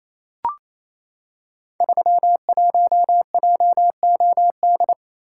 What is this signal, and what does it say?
Morse code, heard as a keyed tone, about 27 wpm, 700 Hz. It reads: 31JOD
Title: Code Groups - 5 Letter:Numbers - 28wpm